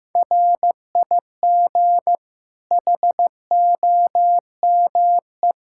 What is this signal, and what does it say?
Morse code, heard as a keyed tone, about 15 wpm, 700 Hz. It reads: RIG HOME
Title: Practice QSO - Exchange 2 - Info - 15wpm